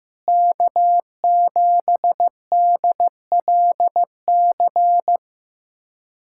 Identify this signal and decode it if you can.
Morse code, heard as a keyed tone, about 15 wpm, 700 Hz. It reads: K7DLC
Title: USA Callsign DE Callsign - 5 Characters - 15wpm